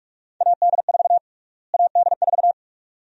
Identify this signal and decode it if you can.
Morse code, heard as a keyed tone, about 46 wpm, 700 Hz. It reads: AD4 AD4
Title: USA Call Sign Prefixes - Repeated 2x - 45wpm